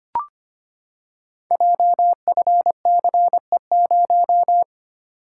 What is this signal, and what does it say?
Morse code, heard as a keyed tone, about 25 wpm, 700 Hz. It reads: JFCE0
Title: Code Groups - 5 Letter:Numbers - 25wpm